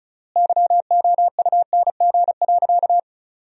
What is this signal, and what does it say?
Morse code, heard as a keyed tone, about 34 wpm, 700 Hz. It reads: YOUNG.
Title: 3x Word-Spacing - Sentences from Top 400 Words - 35wpm